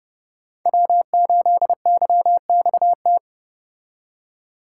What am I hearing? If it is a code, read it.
Morse code, heard as a keyed tone, about 29 wpm, 700 Hz. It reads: W8YXT